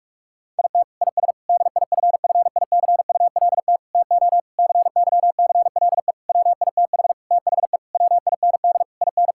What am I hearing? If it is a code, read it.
Morse code, heard as a keyed tone, about 41 wpm, 700 Hz. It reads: IT IS DIFFICULT TO CYCLE WITH THE WIND IN
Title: Sentences from Top 1000 Words - Review Part 6 - 45wpm